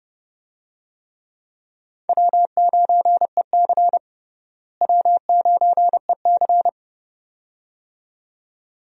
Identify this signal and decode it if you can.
Morse code, heard as a keyed tone, about 31 wpm, 700 Hz. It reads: W9EC W9EC
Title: USA Call Signs - 4 Characters - Repeated 2x - 30wpm